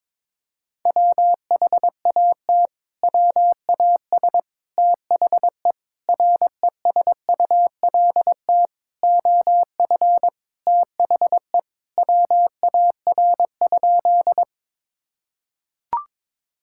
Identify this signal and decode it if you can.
Morse code, heard as a keyed tone, about 22 wpm, 700 Hz. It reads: WHAT WAS THE RESULT OF THE WAR?
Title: Sentences from Top 1000 Words - Review Part 5 - 22wpm